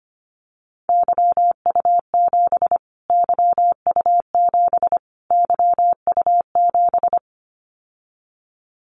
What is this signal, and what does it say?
Morse code, heard as a keyed tone, about 25 wpm, 700 Hz. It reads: YU7 YU7 YU7
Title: Callsign Prefixes for Top 25 DXCC - Repeated 3x - 25wpm